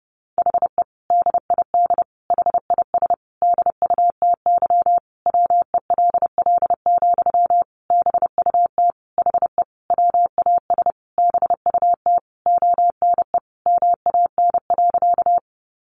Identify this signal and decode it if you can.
Morse code, heard as a keyed tone, about 28 wpm, 700 Hz. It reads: HE DID HIS DUTY WELL, BUT HE WAS BUT ONE MAN.
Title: Sentences from Top 1500 Words 30wpm